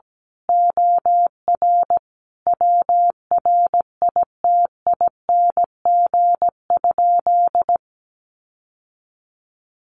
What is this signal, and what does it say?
Morse code, heard as a keyed tone, about 17 wpm, 700 Hz. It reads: OR WRITING?
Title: Sentences from Top 1200 Words 17wpm